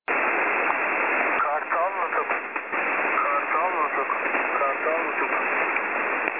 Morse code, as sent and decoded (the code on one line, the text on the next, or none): none